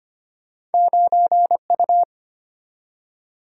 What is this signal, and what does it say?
Morse code, heard as a keyed tone, about 34 wpm, 700 Hz. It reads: 9U